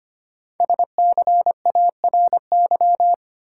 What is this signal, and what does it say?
Morse code, heard as a keyed tone, about 25 wpm, 700 Hz. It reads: SCARY
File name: 5 Letter Words - First 3 Letters Given - 25wpm